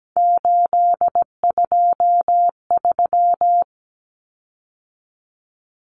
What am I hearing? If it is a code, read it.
Morse code, heard as a keyed tone, about 17 wpm, 700 Hz. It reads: 823
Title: Common Cut Numbers - 3-Digit 17wpm